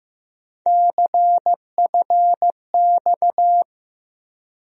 Morse code, heard as a keyed tone, about 15 wpm, 700 Hz. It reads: CFX
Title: USA Call Sign Postfixes - 3 Letters 15wpm